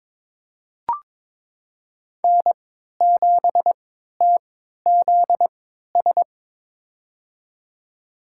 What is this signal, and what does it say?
Morse code, heard as a keyed tone, about 22 wpm, spaced as for 12 wpm, 700 Hz. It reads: N7TZS
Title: USA Call Signs - 5 Characters - 3x Character Spacing - 22wpm